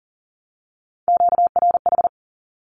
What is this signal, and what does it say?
Morse code, heard as a keyed tone, about 40 wpm, 700 Hz. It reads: QRH
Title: Q-Codes 40wpm